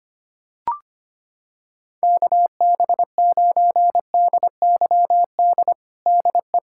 Morse code, heard as a keyed tone, about 25 wpm, 700 Hz. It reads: KB9DYD DE
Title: USA Callsign DE Callsign - 6 Characters - 25wpm